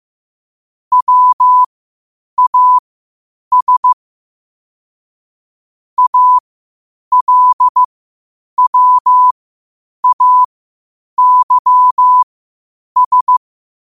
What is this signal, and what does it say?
Morse code, heard as a keyed tone, about 15 wpm, spaced as for 8 wpm, 1000 Hz. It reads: WAS ALWAYS
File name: qotdfiles / 8wpm